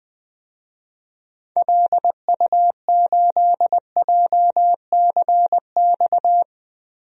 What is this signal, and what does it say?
Morse code, heard as a keyed tone, about 20 wpm, 700 Hz. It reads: LU8JCX
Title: DXCC-100 Call Signs - 6 Characters - Repeated 3x - 20wpm